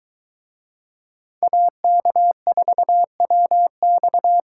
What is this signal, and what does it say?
Morse code, heard as a keyed tone, about 23 wpm, 700 Hz. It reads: AK4WX